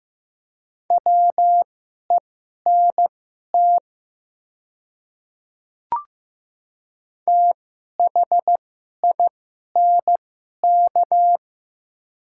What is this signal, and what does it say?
Morse code, heard as a keyed tone, about 15 wpm, spaced as for 11 wpm, 700 Hz. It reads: WENT THINK
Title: 2x Character Spacing - Top 200 Words - 15wpm